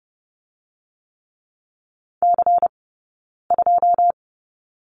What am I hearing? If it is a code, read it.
Morse code, heard as a keyed tone, about 30 wpm, 700 Hz. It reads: C 2
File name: course / 30:10wpm / 099 Lesson 17d - Letter u - Two Characters 30wpm